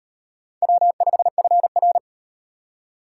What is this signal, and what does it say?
Morse code, heard as a keyed tone, about 35 wpm, 700 Hz. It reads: W5FR